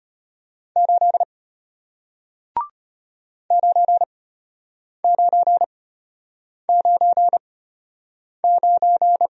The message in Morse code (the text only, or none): 8 9999